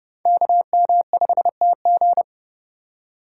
Morse code, heard as a keyed tone, about 37 wpm, 700 Hz. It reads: KM5TG
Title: Speed Racing - USA 2x2 Call Signs 30wpm